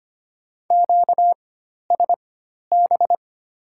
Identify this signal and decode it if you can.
Morse code, heard as a keyed tone, about 25 wpm, spaced as for 12 wpm, 700 Hz. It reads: QSB
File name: Top 90 QSO Elements - 4x Character Spacing - 25wpm